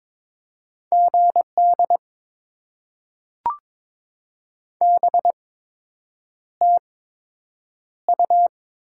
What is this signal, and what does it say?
Morse code, heard as a keyed tone, about 22 wpm, spaced as for 8 wpm, 700 Hz. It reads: GD BTU
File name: Top 90 QSO Elements - 8x Character Spacing - 22wpm